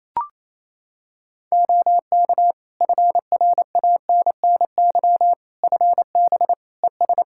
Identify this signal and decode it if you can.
Morse code, heard as a keyed tone, about 28 wpm, 700 Hz. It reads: OK FRANNY FB ES
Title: Practice QSO - Exchange 2 - Info - 28wpm